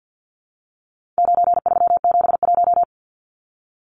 Morse code, heard as a keyed tone, about 50 wpm, 700 Hz. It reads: 9371
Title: Numbers - 4-Digit 50wpm